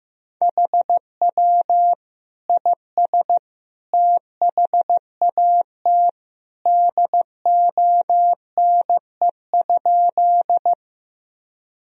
Morse code, heard as a keyed tone, about 16 wpm, 700 Hz. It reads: HW IS THAT DONE?
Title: Sentences from Top 500 Words and Abbreviations 15wpm